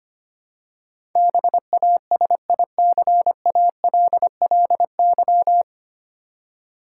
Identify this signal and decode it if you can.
Morse code, heard as a keyed tone, about 25 wpm, 700 Hz. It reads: BASICALLY
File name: Nine Letter Words - 25wpm